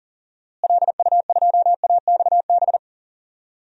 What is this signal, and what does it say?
Morse code, heard as a keyed tone, about 38 wpm, 700 Hz. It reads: LU2AXB
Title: DXCC-100 Call Signs - 6 Characters - Repeated 3x - 40wpm